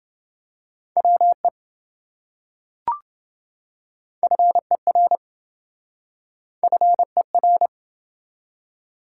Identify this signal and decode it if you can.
Morse code, heard as a keyed tone, about 27 wpm, 700 Hz. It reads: WE FER FER